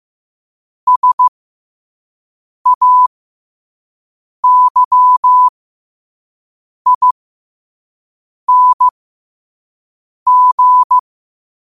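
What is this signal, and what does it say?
Morse code, heard as a keyed tone, about 15 wpm, spaced as for 5 wpm, 1000 Hz. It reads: SAYING
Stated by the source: Morse code quote of the day sent at 5 WPM.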